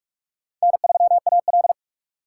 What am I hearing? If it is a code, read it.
Morse code, heard as a keyed tone, about 46 wpm, 700 Hz. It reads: N3AL
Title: USA Call Signs - 4 Characters - 6x Character Spacing - 45wpm